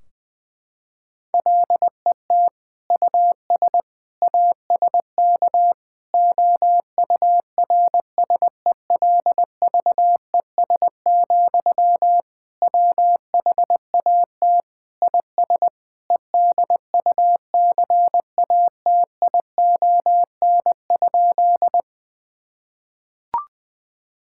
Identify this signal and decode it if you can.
Morse code, heard as a keyed tone, about 20 wpm, 700 Hz. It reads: LET US ASK OURSELVES, WHAT IS EDUCATION?
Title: Sentences from Top 1600 Words 20wpm